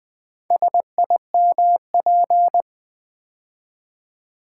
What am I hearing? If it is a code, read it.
Morse code, heard as a keyed tone, about 20 wpm, 700 Hz. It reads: SIMP
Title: Build A Word - 2 Letters At A Time - 6-Letter Words - 20wpm